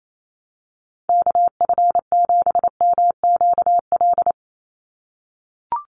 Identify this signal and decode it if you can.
Morse code, heard as a keyed tone, about 28 wpm, 700 Hz. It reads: KF7MQL
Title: USA Call Signs 28wpm